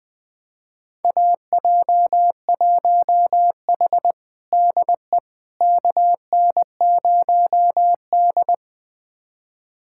Morse code, heard as a keyed tone, about 21 wpm, 700 Hz. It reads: AJ1H DE KN0D
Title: USA Callsign DE Callsign - 4 Characters - 20wpm